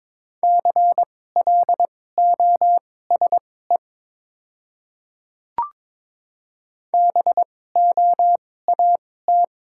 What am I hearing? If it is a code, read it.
Morse code, heard as a keyed tone, about 22 wpm, spaced as for 16 wpm, 690 Hz. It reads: CLOSE BOAT
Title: 2x Character Spacing - Top 500 Words - 22wpm